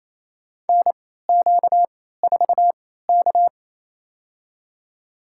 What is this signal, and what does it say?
Morse code, heard as a keyed tone, about 28 wpm, spaced as for 16 wpm, 700 Hz. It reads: NQ4K
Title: USA Call Signs - 4 Characters - 3x Character Spacing - 28wpm